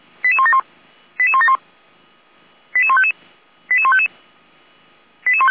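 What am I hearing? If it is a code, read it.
Five-tone selective call (ZVEI), heard as quick runs of five tones, tones of about 70 ms. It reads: 88171 88171 88155 88155
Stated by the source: ZVEI five-tone call received via SDR, fire brigade Orlamünde (feuerwehr-orlamuende) receiver.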